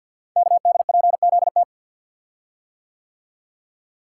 Morse code, heard as a keyed tone, about 51 wpm, 700 Hz. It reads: KDPZT
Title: Code Groups - 5 Letters - 50wpm